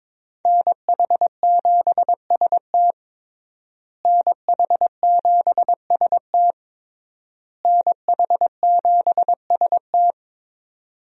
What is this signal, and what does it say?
Morse code, heard as a keyed tone, about 22 wpm, 700 Hz. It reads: NH7ST NH7ST NH7ST
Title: USA Call Signs - 5 Characters - Repeated 3x - 22wpm